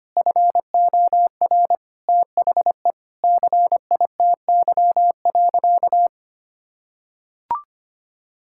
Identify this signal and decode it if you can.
Morse code, heard as a keyed tone, about 25 wpm, 700 Hz. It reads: FOR THE CITY.
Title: Sentences from Top 1600 Words 25wpm